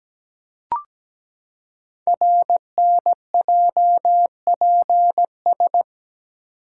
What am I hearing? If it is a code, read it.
Morse code, heard as a keyed tone, about 17 wpm, 700 Hz. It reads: RNJPS
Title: Code Groups - 5 Letters - 17wpm